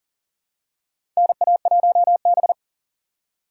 Morse code, heard as a keyed tone, about 38 wpm, 700 Hz. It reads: NA1B
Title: USA Call Signs - 4 Characters - 3x Character Spacing - 40wpm